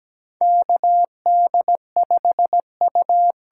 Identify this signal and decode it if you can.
Morse code, heard as a keyed tone, about 17 wpm, 700 Hz. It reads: KD5U
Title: USA Call Signs - 6 Characters - 8x Character Spacing - 17wpm